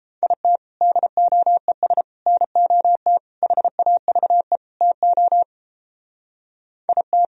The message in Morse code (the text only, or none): IT DOES NOT HAVE TO IT